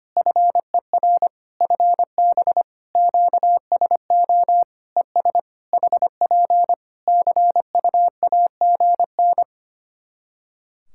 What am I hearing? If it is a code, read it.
Morse code, heard as a keyed tone, about 25 wpm, 700 Hz. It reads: FER FB QSO ES HP CUAGN
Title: Practice QSO - Exchange 4 - Ending - 25wpm